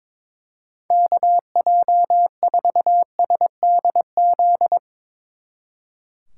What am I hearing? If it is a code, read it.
Morse code, heard as a keyed tone, about 24 wpm, 700 Hz. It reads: KJ4SDZ
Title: Speed Racing - USA 2x3 Call Signs 22wpm